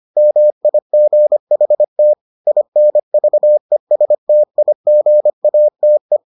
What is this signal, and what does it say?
Morse code, heard as a keyed tone, about 25 wpm, 600 Hz. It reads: MIGHT INVESTIGATE